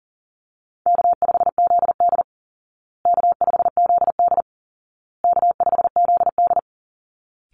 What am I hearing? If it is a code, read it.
Morse code, heard as a keyed tone, about 40 wpm, 700 Hz. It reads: K5ZD K5ZD K5ZD
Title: Easy Call Signs Repeated 3x - 40wpm